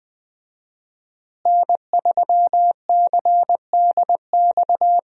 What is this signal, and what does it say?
Morse code, heard as a keyed tone, about 20 wpm, 700 Hz. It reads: N3CDX
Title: USA Call Signs - 5 Characters - Repeated 2x - 20wpm